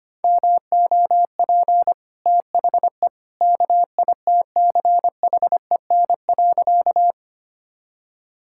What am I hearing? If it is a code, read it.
Morse code, heard as a keyed tone, about 25 wpm, 690 Hz. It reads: MOP THE KITCHEN.
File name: Easy Sentences with Hints - 25wpm